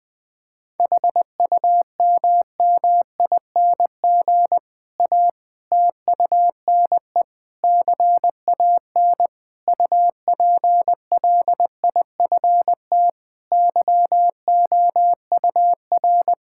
Morse code, heard as a keyed tone, about 21 wpm, 700 Hz. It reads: HUMMING A TUNE CAN UPLIFT YOUR
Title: Difficult Sentences with Hints - 20wpm